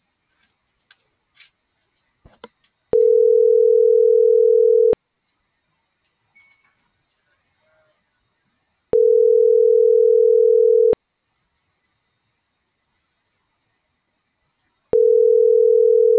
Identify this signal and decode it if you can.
Ringback tone (North America), 440 + 480 Hz.
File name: CENTRAL TERRESTRE / out jessamynvillafuerte